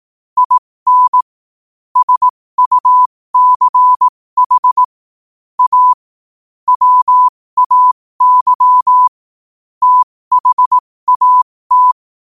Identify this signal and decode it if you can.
Morse code, heard as a keyed tone, about 18 wpm, spaced as for 15 wpm, 1000 Hz. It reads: IN SUCH A WAY THAT
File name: qotdfiles / 15wpm